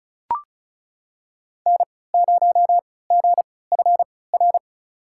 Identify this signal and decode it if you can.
Morse code, heard as a keyed tone, about 34 wpm, spaced as for 20 wpm, 700 Hz. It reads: N0GFR
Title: USA Call Signs - 5 Characters - 3x Character Spacing - 35wpm